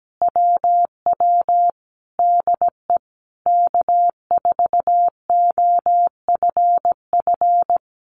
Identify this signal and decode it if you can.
Morse code, heard as a keyed tone, about 17 wpm, 700 Hz. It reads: WW DE K4OFF
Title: Common QSO Phrases Encore 17wpm